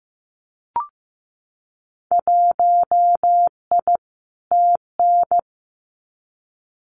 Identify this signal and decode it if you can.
Morse code, heard as a keyed tone, about 15 wpm, 700 Hz. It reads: 1I TN
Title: Winter Field Day Exchanges Encore 15wpm